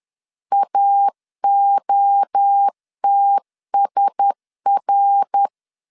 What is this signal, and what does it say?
Morse code, heard as a keyed tone, about 10 wpm, 800 Hz. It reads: AOTSR